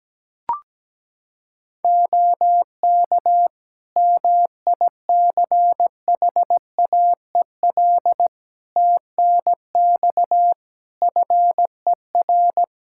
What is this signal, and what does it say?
Morse code, heard as a keyed tone, about 17 wpm, 700 Hz. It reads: OK MICHAEL TNX FER
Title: Practice QSO - Exchange 4 - Ending - 17wpm